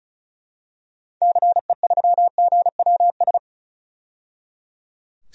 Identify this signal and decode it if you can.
Morse code, heard as a keyed tone, about 36 wpm, 700 Hz. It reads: CE3GWS
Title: DXCC-100 Call Signs - 6 Characters - 35wpm